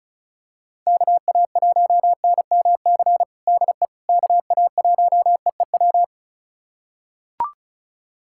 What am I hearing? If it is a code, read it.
Morse code, heard as a keyed tone, about 36 wpm, 700 Hz. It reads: KA1NMC DE KA1EEW
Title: USA Callsign DE Callsign 35wpm